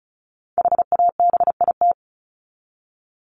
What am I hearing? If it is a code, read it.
Morse code, heard as a keyed tone, about 34 wpm, 700 Hz. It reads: HABIT
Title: Top 1751-2000 Words 35wpm